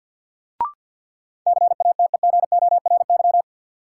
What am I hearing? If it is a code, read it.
Morse code, heard as a keyed tone, about 47 wpm, 700 Hz. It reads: CATEGORY
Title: Sentences from Top 1000 Words - Review Part 7 - New Word Drill - 50wpm